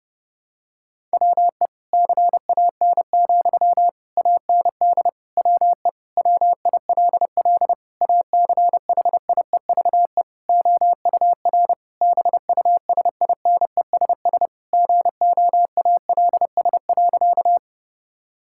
Morse code, heard as a keyed tone, about 30 wpm, 700 Hz. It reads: WE CAN, AND WE WILL ACHIEVE OUR BUSINESS GOALS.
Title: Sentences from Top 1000 Words - Review Part 6 - 30wpm